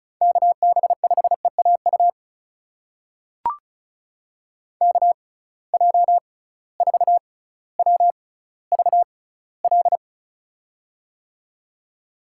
Morse code, heard as a keyed tone, about 36 wpm, spaced as for 18 wpm, 700 Hz. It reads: KB5EAU KJ4WVL
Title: USA Call Signs - 6 Characters - 6x Character Spacing - 35wpm